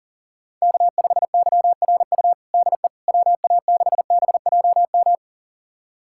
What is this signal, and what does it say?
Morse code, heard as a keyed tone, about 40 wpm, 700 Hz. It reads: K5YRU DE WA6BJM
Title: USA Callsign DE Callsign 40wpm